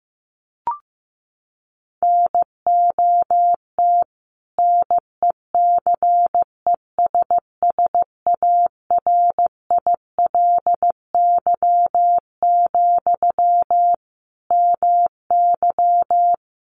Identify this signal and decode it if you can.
Morse code, heard as a keyed tone, about 15 wpm, 700 Hz. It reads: NOT NECESSARILY, MY
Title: Sentences from Top 1400 Words 15wpm